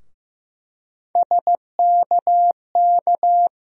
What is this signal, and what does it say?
Morse code, heard as a keyed tone, about 15 wpm, 700 Hz. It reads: SKK
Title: USA Call Sign Postfixes - 3 Letters 15wpm